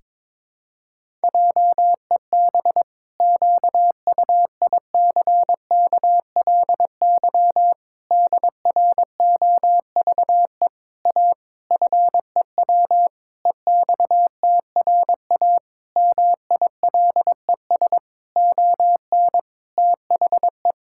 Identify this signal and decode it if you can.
Morse code, heard as a keyed tone, about 22 wpm, 700 Hz. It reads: JEB QUICKLY DROVE A FEW EXTRA MILES ON THE
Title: Pangram Challenge v2 - 22wpm